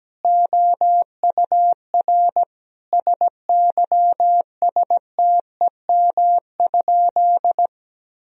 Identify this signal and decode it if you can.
Morse code, heard as a keyed tone, about 17 wpm, 700 Hz. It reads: OUR SYSTEM?
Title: Sentences from Top 1000 Words - Review Part 3 - 17wpm